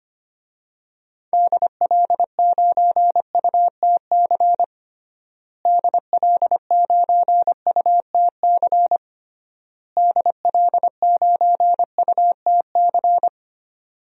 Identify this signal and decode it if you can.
Morse code, heard as a keyed tone, about 25 wpm, 700 Hz. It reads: DL9UTC DL9UTC DL9UTC
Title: DXCC-100 Call Signs - 6 Characters - Repeated 3x - 25wpm